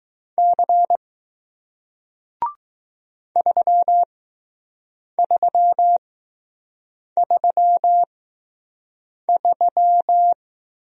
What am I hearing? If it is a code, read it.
Morse code, heard as a keyed tone, about 21 wpm, spaced as for 6 wpm, 700 Hz. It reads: C 3333